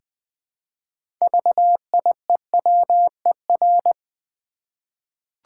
Morse code, heard as a keyed tone, about 20 wpm, 700 Hz. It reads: VIEWER
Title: 6 Letter Words - First 3 Letters Given - 20wpm